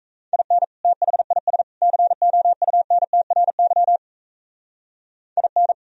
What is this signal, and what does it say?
Morse code, heard as a keyed tone, about 37 wpm, 700 Hz. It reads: IN THIS COUNTRY IN